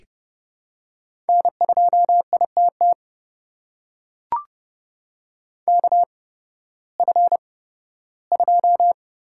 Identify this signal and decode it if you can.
Morse code, heard as a keyed tone, about 30 wpm, spaced as for 16 wpm, 700 Hz. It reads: N2ITT KF2
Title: USA Call Signs - 5 Characters - 8x Character Spacing - 30wpm